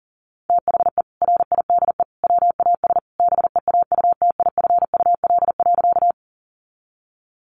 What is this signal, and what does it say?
Morse code, heard as a keyed tone, about 40 wpm, 700 Hz. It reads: THE RIDE WAS BEAUTIFUL.
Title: Sentences from Top 1200 Words 40wpm